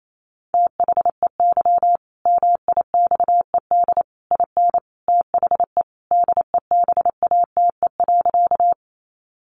Morse code, heard as a keyed tone, about 28 wpm, 700 Hz. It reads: THEY MIXED IN THE DEBATE.
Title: Sentences from Top 1500 Words 28wpm